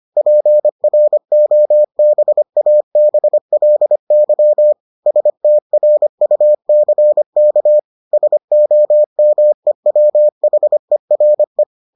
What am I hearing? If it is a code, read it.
Morse code, heard as a keyed tone, about 25 wpm, 600 Hz. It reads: PROBABLY STRUCK SOMEWHERE